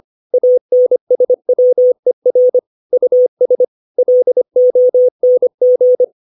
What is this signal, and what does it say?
Morse code, heard as a keyed tone, about 25 wpm, 500 Hz. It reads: ANSWER US LONG
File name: 632 words / 632 Words 25WPM